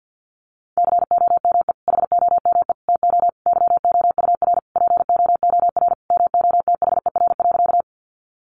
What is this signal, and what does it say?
Morse code, heard as a keyed tone, about 50 wpm, 700 Hz. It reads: COME HOME TO YOUR POOR MOTHER.
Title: Sentences from Top 1000 Words 50wpm